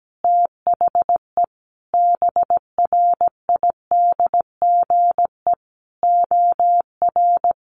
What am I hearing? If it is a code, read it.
Morse code, heard as a keyed tone, about 17 wpm, 700 Hz. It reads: THE BRIDGE OR
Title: Sentences from Top 1300 Words 17wpm